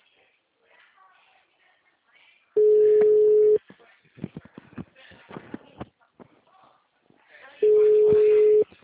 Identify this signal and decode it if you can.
Ringback tone (Europe, 425 Hz), with speech.
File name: out- -2001- - - .67282